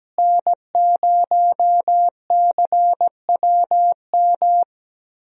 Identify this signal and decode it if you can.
Morse code, heard as a keyed tone, about 17 wpm, 700 Hz. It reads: N0CWM
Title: USA Call Signs - 5 Characters - Repeated 3x - 17wpm